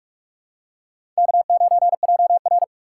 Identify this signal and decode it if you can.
Morse code, heard as a keyed tone, about 46 wpm, 700 Hz. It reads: K9JR
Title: USA Call Signs - 2x Character Spacing - 45wpm